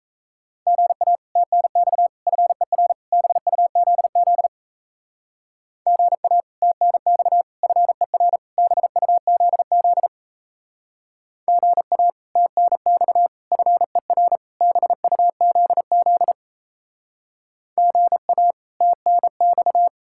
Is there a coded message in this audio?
Yes: Morse code from 0.7 to 19.9 s, heard as a keyed tone, about 36 wpm, 700 Hz. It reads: GA TNX FER BUZZ GA TNX FER BUZZ GA TNX FER BUZZ GA TNX